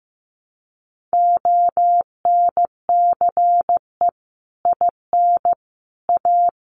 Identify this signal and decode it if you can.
Morse code, heard as a keyed tone, about 15 wpm, 700 Hz. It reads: ONCE IN A
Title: Sets of 4 Words 15wpm